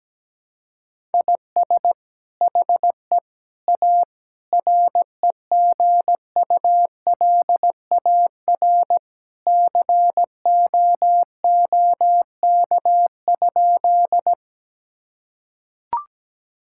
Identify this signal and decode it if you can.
Morse code, heard as a keyed tone, about 17 wpm, 700 Hz. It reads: IS HE A REGULAR COOK?
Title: Sentences from Top 1000 Words - Review Part 6 - 17wpm